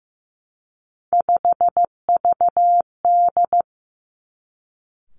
Morse code, heard as a keyed tone, about 15 wpm, 700 Hz. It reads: 5VD
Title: Three Challenging Letter-Number Combinations 15wpm